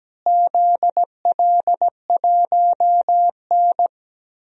Morse code, heard as a keyed tone, about 17 wpm, 700 Hz. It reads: ZL1N
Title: DXCC-100 Call Signs - 4 Characters - Repeated 3x - 17wpm